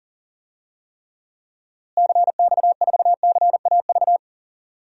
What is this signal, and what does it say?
Morse code, heard as a keyed tone, about 40 wpm, 700 Hz. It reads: CX4CAV
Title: DXCC-100 Call Signs - 6 Characters - Repeated 3x - 40wpm